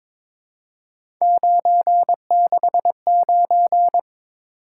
Morse code, heard as a keyed tone, about 22 wpm, 700 Hz. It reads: 969
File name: practice / 22wpm / Three Letter-Number Combinations - 22wpm